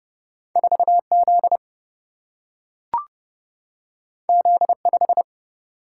Morse code, heard as a keyed tone, about 30 wpm, 700 Hz. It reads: 4 Z Z 5
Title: Single-Letter Single-Number 30wpm